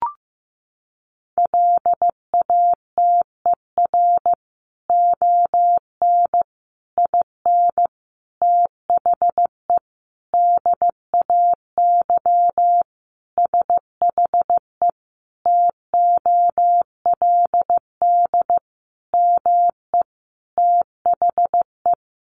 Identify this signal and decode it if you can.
Morse code, heard as a keyed tone, about 15 wpm, 700 Hz. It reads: LATER ON IN THE DAY SHE TOLD ME THE
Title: Sentences from Top 700 Words 15wpm